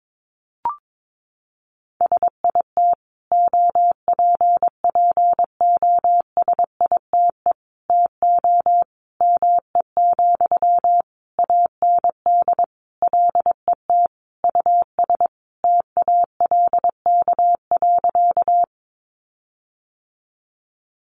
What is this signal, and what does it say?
Morse code, heard as a keyed tone, about 22 wpm, 700 Hz. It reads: SIT OPPOSITE TO ME, AND LET US TALK.
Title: Sentences from Top 1300 Words 22wpm